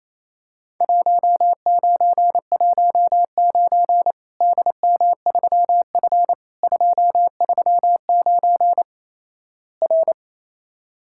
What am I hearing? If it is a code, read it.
Morse code, heard as a keyed tone, about 28 wpm, 700 Hz. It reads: 1919 DM3F 239 R
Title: Worked All Europe DX Contest - QTC Exchange - 28wpm